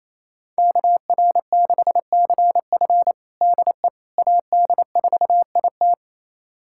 Morse code, heard as a keyed tone, about 28 wpm, 700 Hz. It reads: KR6CF DE AD4IT
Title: USA Callsign DE Callsign - 5 Characters - 28wpm